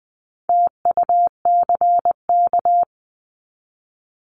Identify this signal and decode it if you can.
Morse code, heard as a keyed tone, about 20 wpm, 700 Hz. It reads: TUCK
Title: 260 Lesson 42e - Words 20-20wpm